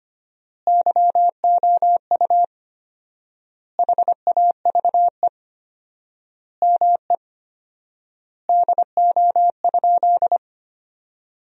Morse code, heard as a keyed tone, about 25 wpm, 700 Hz. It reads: YOU HAVE ME DO?
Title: 4x Word-Spacing - Sentences from Top 200 Words - 25wpm